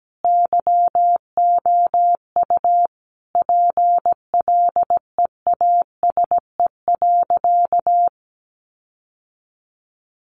Morse code, heard as a keyed tone, about 17 wpm, 700 Hz. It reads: YOU PLEASE.
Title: Sentences from Top 1500 Words 17wpm